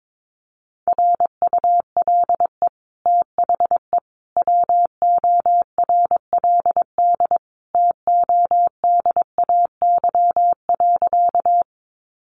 Morse code, heard as a keyed tone, about 22 wpm, 700 Hz. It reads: RULE THE WORLD TODAY.
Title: Sentences from Top 600 Words 22wpm